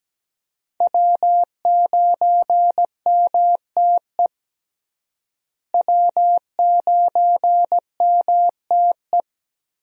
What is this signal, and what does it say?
Morse code, heard as a keyed tone, about 17 wpm, 700 Hz. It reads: W9MTE W9MTE
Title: USA Call Signs - 5 Characters - Repeated 2x - 17wpm